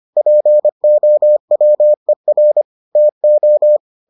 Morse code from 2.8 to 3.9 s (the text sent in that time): TO